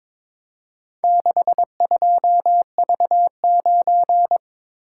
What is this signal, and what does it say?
Morse code, heard as a keyed tone, about 23 wpm, 700 Hz. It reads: 62V9
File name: practice / 22wpm / Four Letter-Number Combinations - 22wpm